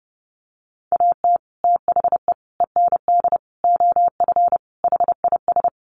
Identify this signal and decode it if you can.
Morse code, heard as a keyed tone, about 29 wpm, 700 Hz. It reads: AT THE END OF HIS
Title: Sets of 5 Words Encore 30wpm